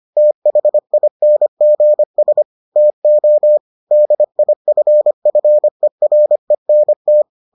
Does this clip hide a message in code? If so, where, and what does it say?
Yes: Morse code from 0.2 to 7.2 s, heard as a keyed tone, about 25 wpm, 600 Hz. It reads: THINGS TO DIFFERENT